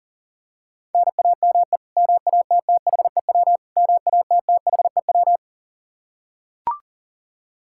Morse code, heard as a keyed tone, about 40 wpm, 700 Hz. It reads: NAME MATTHEW MATTHEW
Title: K1USN SST QSO Practice 40wpm